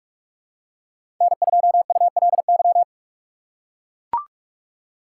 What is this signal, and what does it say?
Morse code, heard as a keyed tone, about 41 wpm, 700 Hz. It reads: N2ULY
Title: USA Call Signs - 5 Characters - 4x Character Spacing - 45wpm